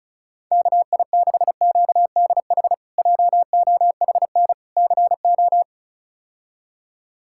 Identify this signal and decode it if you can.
Morse code, heard as a keyed tone, about 36 wpm, 700 Hz. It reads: KI6QDH JOHN CO
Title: K1USN SST QSO Practice - 35wpm